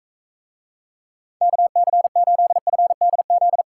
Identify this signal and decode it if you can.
Morse code, heard as a keyed tone, about 41 wpm, 700 Hz. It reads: KC8FDZ